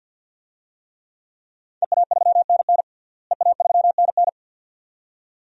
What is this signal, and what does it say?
Morse code, heard as a keyed tone, about 47 wpm, 700 Hz. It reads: EA3NN EA3NN
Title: DXCC-100 Call Signs - 5 Characters - Repeated 2x - 50wpm